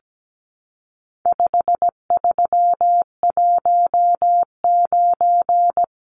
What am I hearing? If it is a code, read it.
Morse code, heard as a keyed tone, about 17 wpm, 700 Hz. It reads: 5319
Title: Numbers - 4-Digit 17wpm